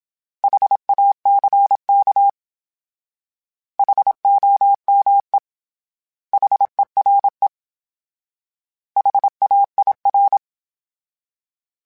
Morse code, heard as a keyed tone, about 27 wpm, 800 Hz. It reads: HACK HOME HERE HAIR
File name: H Prefixes / H mp3s